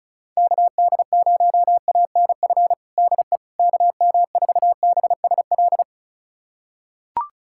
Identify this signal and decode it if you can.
Morse code, heard as a keyed tone, about 34 wpm, 690 Hz. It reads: KD0ANF DE KM4BSL
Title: USA Callsign DE Callsign - 6 Characters - 35wpm